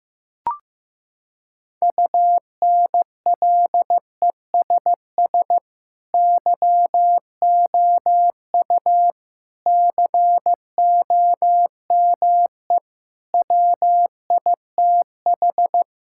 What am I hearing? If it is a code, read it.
Morse code, heard as a keyed tone, about 15 wpm, 700 Hz. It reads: UNLESS YOU COME WITH
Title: Sentences from Top 1000 Words - Review Part 6 - 15wpm